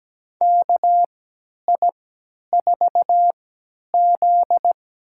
Morse code, heard as a keyed tone, about 17 wpm, spaced as for 10 wpm, 700 Hz. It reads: KI4Z
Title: USA Call Signs - 3x Character Spacing - 17wpm